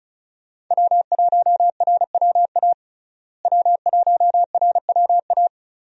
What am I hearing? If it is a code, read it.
Morse code, heard as a keyed tone, about 34 wpm, 700 Hz. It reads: W1RWA W1RWA
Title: USA Call Signs - 5 Characters - Repeated 2x - 35wpm